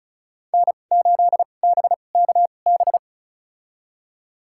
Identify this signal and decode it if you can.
Morse code, heard as a keyed tone, about 36 wpm, spaced as for 25 wpm, 700 Hz. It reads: N8BKB
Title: USA Call Signs - 5 Characters - 2x Character Spacing - 35wpm